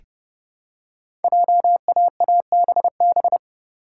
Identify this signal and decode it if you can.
Morse code, heard as a keyed tone, about 30 wpm, 700 Hz. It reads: JAABB
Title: Code Groups - 5 Letters - 30wpm